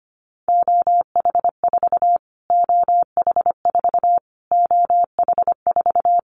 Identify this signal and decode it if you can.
Morse code, heard as a keyed tone, about 25 wpm, 700 Hz. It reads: OH4 OH4 OH4
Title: Callsign Prefixes for Top 25 DXCC - Repeated 3x - 25wpm